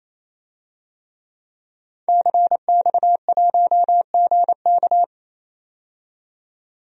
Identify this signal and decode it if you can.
Morse code, heard as a keyed tone, about 27 wpm, 700 Hz. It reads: CX1GK
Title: DXCC-100 Call Signs - 5 Characters - 28wpm